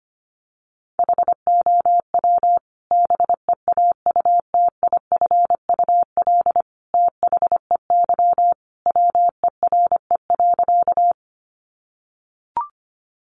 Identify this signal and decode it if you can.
Morse code, heard as a keyed tone, about 25 wpm, 700 Hz. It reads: HOW BEAUTIFUL THEY WERE.
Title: Sentences from Top 900 Words 25wpm